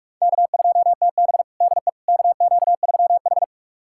Morse code, heard as a keyed tone, about 46 wpm, 700 Hz. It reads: K2TB DE KQ3H
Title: USA Callsign DE Callsign - 4 Characters - 45wpm